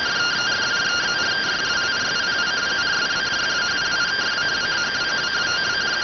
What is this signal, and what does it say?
Two-tone teleprinter signal, about 170 Hz shift.